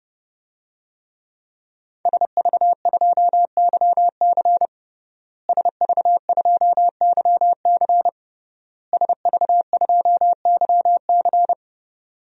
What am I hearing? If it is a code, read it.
Morse code, heard as a keyed tone, about 30 wpm, 700 Hz. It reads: SV2YC SV2YC SV2YC
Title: DXCC-100 Call Signs - 5 Characters - Repeated 3x - 30wpm